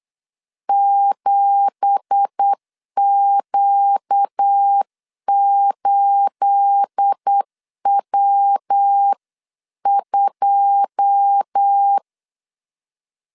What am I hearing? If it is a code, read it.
Morse code, heard as a keyed tone, about 8 wpm, 800 Hz. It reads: 7Q8W2
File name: 8wpm